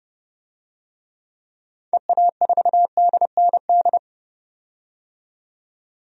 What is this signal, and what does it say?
Morse code, heard as a keyed tone, about 30 wpm, 690 Hz. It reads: EA4DND
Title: DXCC-100 Call Signs - 6 Characters - 30wpm